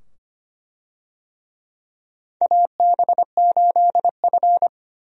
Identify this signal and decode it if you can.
Morse code, heard as a keyed tone, about 25 wpm, 700 Hz. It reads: AB8F
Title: USA Call Signs - 4 Characters - 25wpm